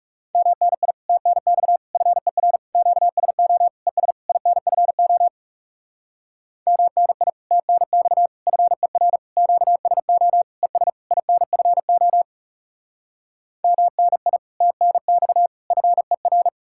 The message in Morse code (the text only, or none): MNI TNX FER QSO ES INFO MNI TNX FER QSO ES INFO MNI TNX FER